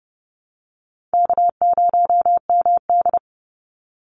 Morse code, heard as a keyed tone, about 30 wpm, 700 Hz. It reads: K0MD
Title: Call Signs - 30wpm